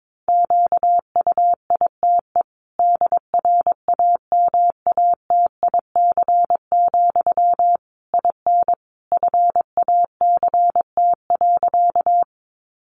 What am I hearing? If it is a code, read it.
Morse code, heard as a keyed tone, about 22 wpm, 700 Hz. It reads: QUITE DRAMATIC, IN FACT.
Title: Sentences from Top 1300 Words 22wpm